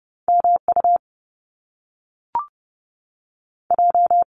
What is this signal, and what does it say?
Morse code, heard as a keyed tone, about 30 wpm, 700 Hz. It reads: M U J
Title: Top Two Letter Combinations in Words 30wpm